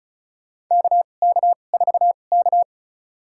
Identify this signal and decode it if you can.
Morse code, heard as a keyed tone, about 36 wpm, spaced as for 25 wpm, 700 Hz. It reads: KK4K
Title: USA Call Signs - 4 Characters - 2x Character Spacing - 35wpm